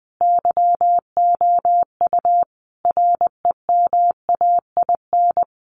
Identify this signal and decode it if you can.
Morse code, heard as a keyed tone, about 20 wpm, 700 Hz. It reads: YOU REMAIN
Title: Sentences from Top 1200 Words 20wpm